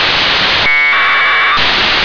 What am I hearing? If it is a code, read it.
Packet radio (AX.25, as in APRS).
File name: 1k2 ooreos